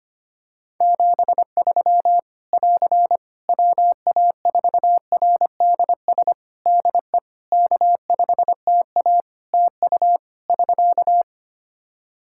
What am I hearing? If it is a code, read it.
Morse code, heard as a keyed tone, about 25 wpm, 700 Hz. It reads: 73 <AR> WA4RDS DE K5TA TU <SK>
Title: Practice QSO - Exchange 4 - Ending - 25wpm